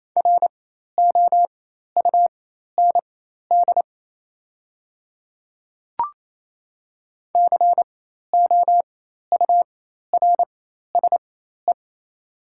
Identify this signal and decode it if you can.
Morse code, heard as a keyed tone, about 28 wpm, spaced as for 13 wpm, 700 Hz. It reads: ROUND COURSE
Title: 4x Character Spacing - Top 500 Words - 28wpm